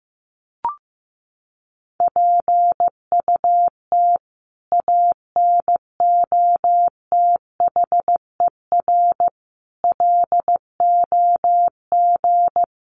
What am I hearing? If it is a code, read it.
Morse code, heard as a keyed tone, about 15 wpm, 700 Hz. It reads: PUT ANOTHER LOG
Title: Sentences from Top 1400 Words 15wpm